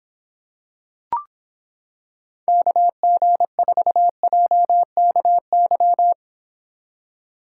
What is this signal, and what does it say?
Morse code, heard as a keyed tone, about 26 wpm, 700 Hz. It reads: KG4JKY